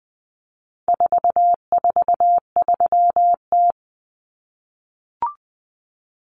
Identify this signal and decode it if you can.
Morse code, heard as a keyed tone, about 20 wpm, 700 Hz. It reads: 443T
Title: Common Cut Numbers - 4-Digit 20wpm